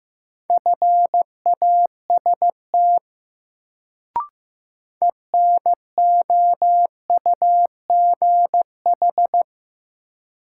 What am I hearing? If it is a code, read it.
Morse code, heard as a keyed tone, about 15 wpm, 700 Hz. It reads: FAST ENOUGH
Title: Sentences from Top 400 Words - New Word Drill - 15wpm